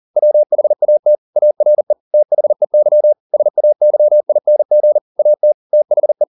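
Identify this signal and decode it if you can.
Morse code, heard as a keyed tone, about 40 wpm, 600 Hz. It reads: WHAT ARE THEY SAYING AT THE
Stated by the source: Morse code at 40 wpm, practice text, CW med Gnister.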